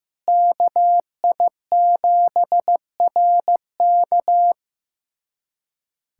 Morse code, heard as a keyed tone, about 15 wpm, 700 Hz. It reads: KI7RK
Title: USA Callsign DE Callsign - 5 Characters - 15wpm